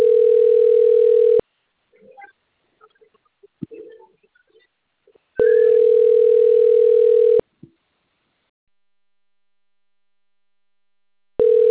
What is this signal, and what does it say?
Ringback tone (North America), 440 + 480 Hz.